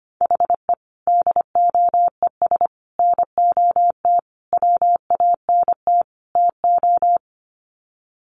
Morse code, heard as a keyed tone, about 25 wpm, 700 Hz. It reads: HE DOES NOT WANT TO
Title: Sets of 5 Words 25wpm